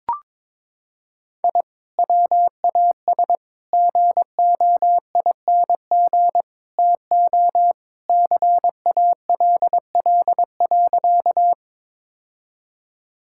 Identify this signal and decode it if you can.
Morse code, heard as a keyed tone, about 22 wpm, 700 Hz. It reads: I WAS GOING TO CALL.
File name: practice / 22wpm / Sentences with 4-Word Phrases - 22wpm